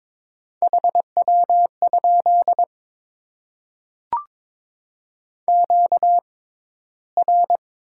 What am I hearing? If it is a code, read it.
Morse code, heard as a keyed tone, about 22 wpm, spaced as for 13 wpm, 700 Hz. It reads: HW? QR
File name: Top 90 QSO Elements - 6x Character Spacing - 22wpm